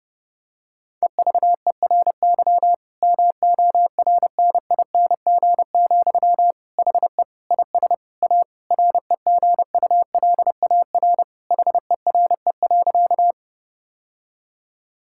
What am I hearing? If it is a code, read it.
Morse code, heard as a keyed tone, about 30 wpm, 700 Hz. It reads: EVERY MORNING, HE IS A REGULAR HERE.
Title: Sentences from Top 1000 Words - Review Part 6 - 30wpm